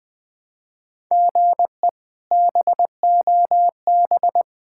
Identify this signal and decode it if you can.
Morse code, heard as a keyed tone, about 21 wpm, 700 Hz. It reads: GE BOB
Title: K1USN SST QSO Practice - 20wpm